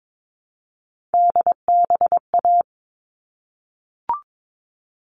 Morse code, heard as a keyed tone, about 22 wpm, 700 Hz. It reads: DBA
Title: All Cut Numbers - 3-Digit 22wpm